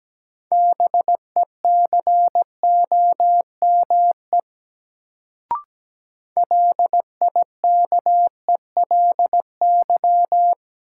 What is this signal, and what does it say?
Morse code, heard as a keyed tone, about 17 wpm, 700 Hz. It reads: BECOME LIKELY
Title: Sentences from Top 600 Words - New Word Drill - 17wpm